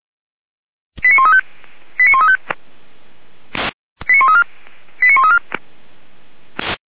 Five-tone selective call (ZVEI), heard as quick runs of five tones, tones of about 70 ms. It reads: 89135 89135 89134 89134